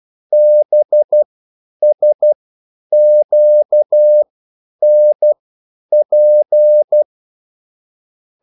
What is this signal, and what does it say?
Morse code, heard as a keyed tone, about 12 wpm, spaced as for 9 wpm, 600 Hz. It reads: BSQNP